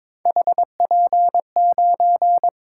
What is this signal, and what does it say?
Morse code, heard as a keyed tone, about 21 wpm, 700 Hz. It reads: HP9
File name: practice / 22wpm / Three Letter-Number Combinations - 22wpm